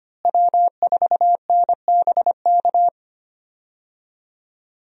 Morse code, heard as a keyed tone, about 25 wpm, 700 Hz. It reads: W4NBK
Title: USA Call Signs - 5 Characters - Repeated 2x - 25wpm